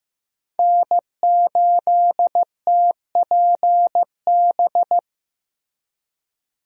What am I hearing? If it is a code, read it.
Morse code, heard as a keyed tone, about 15 wpm, 700 Hz. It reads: N8TPB
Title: USA Call Signs - 5 Characters - Repeated 3x - 15wpm